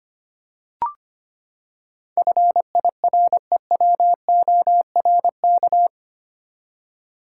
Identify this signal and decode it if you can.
Morse code, heard as a keyed tone, about 25 wpm, 700 Hz. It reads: FIREWORK
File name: Eight Letter Words - 25wpm